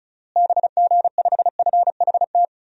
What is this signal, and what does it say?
Morse code, heard as a keyed tone, about 34 wpm, 700 Hz. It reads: BG5FHT
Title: DXCC-100 Call Signs - 6 Characters - Repeated 3x - 35wpm